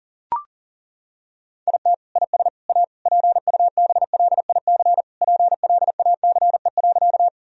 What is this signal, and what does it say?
Morse code, heard as a keyed tone, about 40 wpm, 700 Hz. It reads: IT IS A PUBLIC PLACE.
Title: Sentences from Top 1000 Words - Review Part 3 - 40wpm